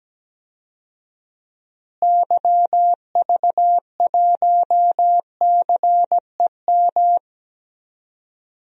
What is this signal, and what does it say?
Morse code, heard as a keyed tone, about 17 wpm, 700 Hz. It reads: YV1CEM
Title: DXCC-100 Call Signs - 6 Characters - Repeated 3x - 17wpm